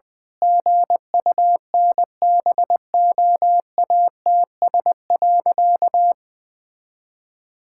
Morse code, heard as a keyed tone, about 20 wpm, 700 Hz. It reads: GUNBOATS.
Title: Pangram Challenge v2 - 20wpm